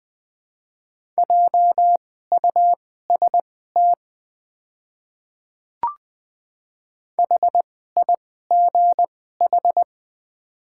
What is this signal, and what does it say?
Morse code, heard as a keyed tone, about 20 wpm, spaced as for 14 wpm, 700 Hz. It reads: JUST HIGH
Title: 2x Character Spacing - Top 300 Words - 20wpm